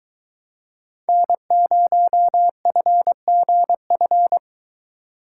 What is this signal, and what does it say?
Morse code, heard as a keyed tone, about 22 wpm, 700 Hz. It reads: N0FGF